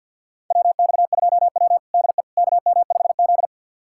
Morse code, heard as a keyed tone, about 51 wpm, 700 Hz. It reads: WX2W DE KM5Z
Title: USA Callsign DE Callsign - 4 Characters - 50wpm